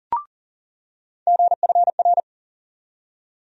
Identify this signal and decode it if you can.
Morse code, heard as a keyed tone, about 37 wpm, 700 Hz. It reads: GFR
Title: USA Call Sign Postfixes - 3 Letters 40wpm